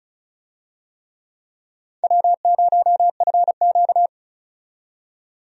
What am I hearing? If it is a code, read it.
Morse code, heard as a keyed tone, about 34 wpm, 700 Hz. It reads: W0FQ
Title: USA Call Signs - 4 Characters - 35wpm